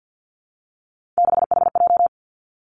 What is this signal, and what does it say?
Morse code, heard as a keyed tone, about 50 wpm, 700 Hz. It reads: 6HJ
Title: Three Challenging Letter-Number Combinations 50wpm